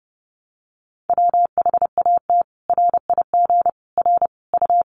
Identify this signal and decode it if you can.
Morse code, heard as a keyed tone, about 30 wpm, 700 Hz. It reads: WHAT RIG R U
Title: Common QSO Phrases 30wpm